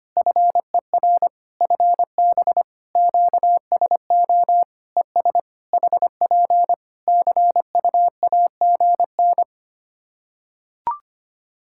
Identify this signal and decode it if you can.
Morse code, heard as a keyed tone, about 25 wpm, 700 Hz. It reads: FER FB QSO ES HP CUAGN
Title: Practice QSO - Exchange 4 - Ending - 25wpm